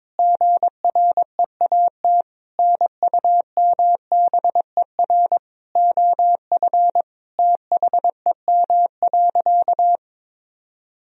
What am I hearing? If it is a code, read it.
Morse code, heard as a keyed tone, about 22 wpm, 700 Hz. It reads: GREAT NUMBER OF THEM.
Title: Sentences from Top 1000 Words - Review Part 1 - 22wpm